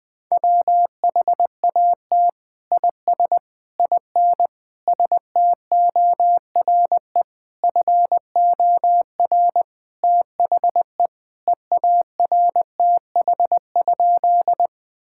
Morse code, heard as a keyed tone, about 20 wpm, 700 Hz. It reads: WHAT IS IN STORE FOR THE EARTH?
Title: Sentences from Top 1000 Words - Review Part 5 - 20wpm